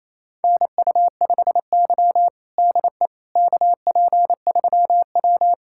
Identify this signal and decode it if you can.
Morse code, heard as a keyed tone, about 28 wpm, 700 Hz. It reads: NU5Y DE KP3W
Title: USA Callsign DE Callsign - 4 Characters - 28wpm